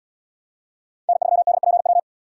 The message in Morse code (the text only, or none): N3DPW